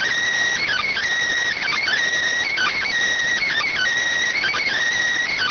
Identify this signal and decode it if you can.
Emergency Alert System (SAME) tones, heard as harsh buzzing bursts. It is heard from 0.6 to 4.8 s.